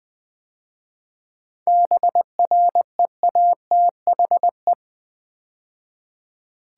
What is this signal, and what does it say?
Morse code, heard as a keyed tone, about 20 wpm, 700 Hz. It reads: BREATHE
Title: 7 Letter Words - First 3 Letters Given - 20wpm